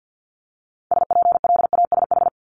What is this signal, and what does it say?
Morse code, heard as a keyed tone, about 50 wpm, 700 Hz. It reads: SPLASH
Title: Multi-letter Phonemes Repeated 3x 50wpm